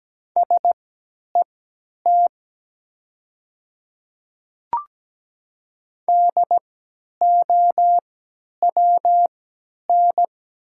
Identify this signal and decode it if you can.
Morse code, heard as a keyed tone, about 17 wpm, spaced as for 10 wpm, 700 Hz. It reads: SET DOWN
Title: 3x Character Spacing - Top 200 Words - 17wpm